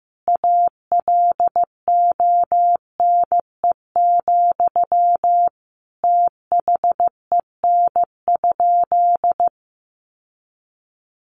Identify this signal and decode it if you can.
Morse code, heard as a keyed tone, about 15 wpm, 700 Hz. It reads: ALONE, THEN?
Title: Sentences from Top 1000 Words 15wpm